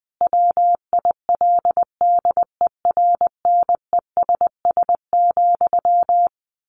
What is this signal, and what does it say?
Morse code, heard as a keyed tone, about 20 wpm, 700 Hz. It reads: WILDERNESS,
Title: A Sound Of Thunder 20wpm